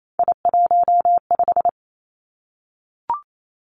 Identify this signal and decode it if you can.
Morse code, heard as a keyed tone, about 28 wpm, 700 Hz. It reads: I15
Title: Three Challenging Letter-Number Combinations 28wpm